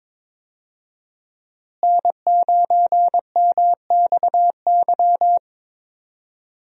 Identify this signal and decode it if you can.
Morse code, heard as a keyed tone, about 22 wpm, 700 Hz. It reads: N9MXY
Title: USA Call Signs - 5 Characters - Repeated 3x - 22wpm